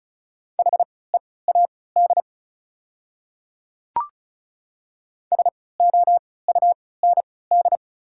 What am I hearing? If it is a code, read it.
Morse code, heard as a keyed tone, about 34 wpm, spaced as for 20 wpm, 700 Hz. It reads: HEAD SOUND
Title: 3x Character Spacing - Top 300 Words - 35wpm